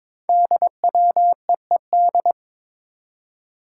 Morse code, heard as a keyed tone, about 22 wpm, 690 Hz. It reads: DWEED
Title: Code Groups - 5 Letter:Numbers - 22wpm